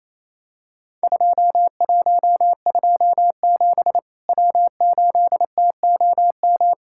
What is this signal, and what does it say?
Morse code, heard as a keyed tone, about 28 wpm, 700 Hz. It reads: 2127 W8TOM
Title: Worked All Europe DX Contest - QTC Exchange - 28wpm